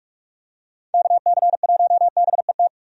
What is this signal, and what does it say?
Morse code, heard as a keyed tone, about 42 wpm, 700 Hz. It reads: KC1BET
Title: USA Call Signs - 6 Characters - Repeated 2x - 45wpm